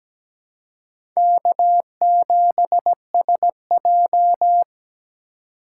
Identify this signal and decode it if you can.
Morse code, heard as a keyed tone, about 17 wpm, 700 Hz. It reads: K7SJ
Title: USA Call Signs - 4 Characters - 2x Character Spacing - 17wpm